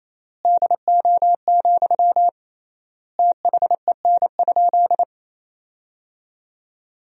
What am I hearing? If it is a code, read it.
Morse code, heard as a keyed tone, about 28 wpm, 700 Hz. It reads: DO, THEN?
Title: 3x Word-Spacing - Sentences from Top 100 Words - 28wpm